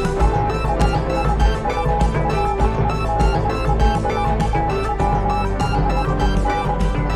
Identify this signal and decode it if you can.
Morse code, heard as a keyed tone, about 20 wpm, 820 Hz. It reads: ASSURED. WHEN